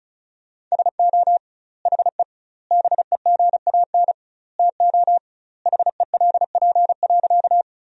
Morse code, heard as a keyed tone, about 34 wpm, 700 Hz. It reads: SO HE BEGAN TO HELP.
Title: 2x Word-Spacing - Sentences from Top 300 Words - 35wpm